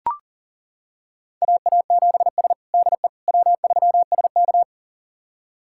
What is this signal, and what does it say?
Morse code, heard as a keyed tone, about 40 wpm, 700 Hz. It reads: AA7S DE W3SK
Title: USA Callsign DE Callsign - 4 Characters - 40wpm